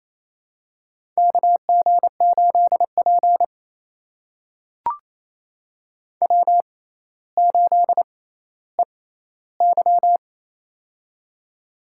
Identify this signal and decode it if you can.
Morse code, heard as a keyed tone, about 28 wpm, spaced as for 14 wpm, 700 Hz. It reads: KG8P W8EY
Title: USA Call Signs - 4 Characters - 6x Character Spacing - 28wpm